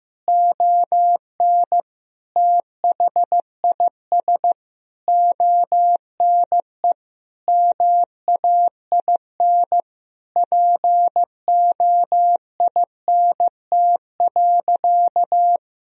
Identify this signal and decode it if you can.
Morse code, heard as a keyed tone, about 15 wpm, 700 Hz. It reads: ON THIS ONE MAIN POINT.
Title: Sentences from Top 1000 Words - Review Part 2 - 15wpm